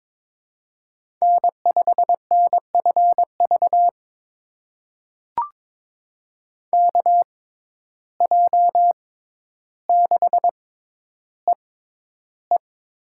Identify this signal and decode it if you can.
Morse code, heard as a keyed tone, about 22 wpm, spaced as for 11 wpm, 700 Hz. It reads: N5NFV KJ6EE
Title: USA Call Signs - 5 Characters - 6x Character Spacing - 22wpm